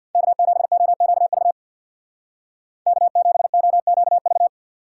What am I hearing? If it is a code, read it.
Morse code, heard as a keyed tone, about 62 wpm, 700 Hz. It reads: K7OQV K7OQV
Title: Speed Racing - USA 1x3 Call Signs 50wpm